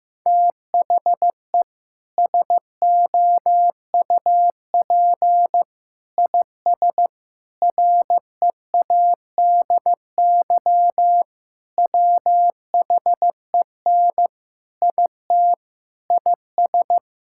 Morse code, heard as a keyed tone, about 16 wpm, 700 Hz. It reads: THE SOUP IS READY WHEN IT IS
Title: Sentences from Top 1600 Words 15wpm